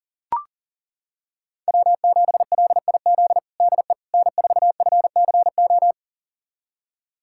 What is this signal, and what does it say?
Morse code, heard as a keyed tone, about 40 wpm, 700 Hz. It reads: W7LIZ DE N4FCO
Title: USA Callsign DE Callsign - 5 Characters - 40wpm